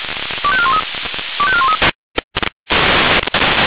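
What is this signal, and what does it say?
Five-tone selective call (ZVEI), heard as quick runs of five tones, tones of about 70 ms. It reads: 25413 25413
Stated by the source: ZVEI five-tone selective call 25218 received on SDR.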